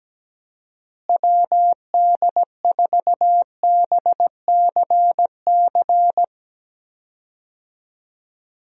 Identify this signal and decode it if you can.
Morse code, heard as a keyed tone, about 17 wpm, 700 Hz. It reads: WD4BCC
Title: USA Call Signs - 6 Characters - Repeated 3x - 17wpm